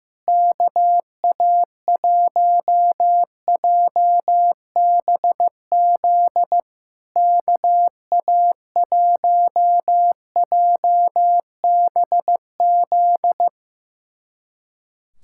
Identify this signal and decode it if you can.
Morse code, heard as a keyed tone, about 15 wpm, 700 Hz. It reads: KA1JBZ KA1JBZ
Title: SOTA QSO Practice 15wpm